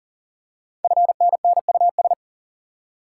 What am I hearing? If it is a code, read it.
Morse code, heard as a keyed tone, about 39 wpm, 700 Hz. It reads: FNNUS
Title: Code Groups - 5 Letters - 40wpm